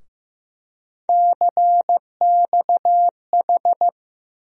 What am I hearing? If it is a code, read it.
Morse code, heard as a keyed tone, about 15 wpm, 700 Hz. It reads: CXH
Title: USA Call Sign Postfixes - 3 Letters 15wpm